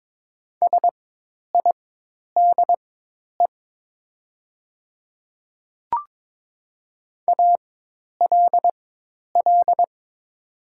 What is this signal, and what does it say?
Morse code, heard as a keyed tone, about 22 wpm, spaced as for 10 wpm, 700 Hz. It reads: SIDE ALL
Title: 4x Character Spacing - Top 100 Words - 22wpm